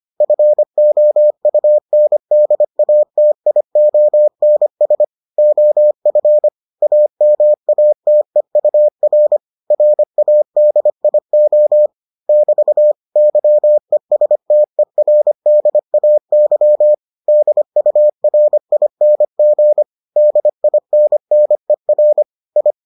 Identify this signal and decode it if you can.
Morse code, heard as a keyed tone, about 25 wpm, 600 Hz. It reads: FOUNDATIONS OF AMATEUR RADIO <BT> YESTERDAY DURING DINNER I